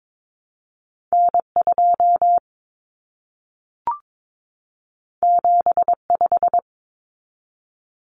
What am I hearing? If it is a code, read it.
Morse code, heard as a keyed tone, about 22 wpm, 700 Hz. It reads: N2 75
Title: Common Cut Numbers - 2-Digit 22wpm